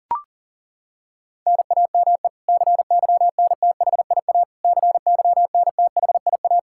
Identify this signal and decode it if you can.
Morse code, heard as a keyed tone, about 40 wpm, 700 Hz. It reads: NAME CYNTHIA CYNTHIA
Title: K1USN SST QSO Practice 40wpm